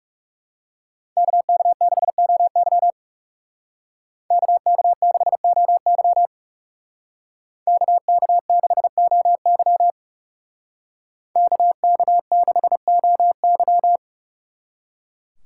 Morse code, heard as a keyed tone, about 37 wpm, 700 Hz. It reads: KK6OY KK6OY KK6OY KK6OY
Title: Speed Racing - USA 2x2 Call Signs 30wpm